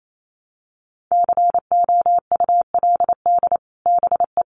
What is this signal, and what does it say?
Morse code, heard as a keyed tone, about 28 wpm, 700 Hz. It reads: COULD BE
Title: Sets of 2 Words 28wpm